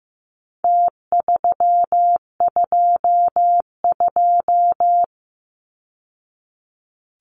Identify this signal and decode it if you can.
Morse code, heard as a keyed tone, about 15 wpm, 700 Hz. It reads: T322
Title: Numbers - Contest Style - 4-Digit 15wpm